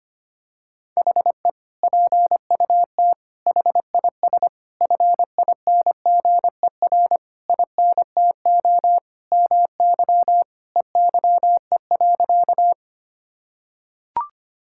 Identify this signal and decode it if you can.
Morse code, heard as a keyed tone, about 25 wpm, 700 Hz. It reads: HE PUT HIS FINGER INTO MY EYE.
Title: Sentences from Top 1600 Words 25wpm